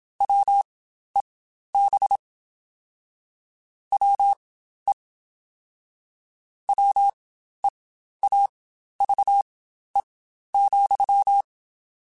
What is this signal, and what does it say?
Morse code, heard as a keyed tone, about 27 wpm, spaced as for 12 wpm, 800 Hz. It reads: WEB WE WEAVE,
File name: Quotes Spaced / Quotes Spaced-mp3s / Deceive sp @25